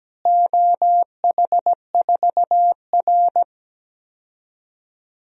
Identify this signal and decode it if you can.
Morse code, heard as a keyed tone, about 17 wpm, 700 Hz. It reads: OH4R
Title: DXCC-100 Call Signs - 4 Characters - Repeated 3x - 17wpm